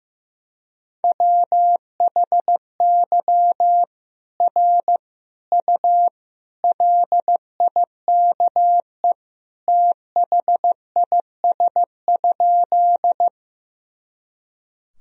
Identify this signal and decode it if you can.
Morse code, heard as a keyed tone, about 16 wpm, 700 Hz. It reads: WHY R U LIKE THIS?
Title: Sentences from Top 200 Words and Abbreviations 15wpm